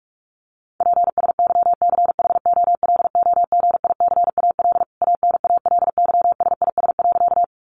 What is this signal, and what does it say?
Morse code, heard as a keyed tone, about 44 wpm, 700 Hz. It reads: PSYCHOLOGICAL ANALYSIS.
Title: Sentences from Top 1400 Words 45wpm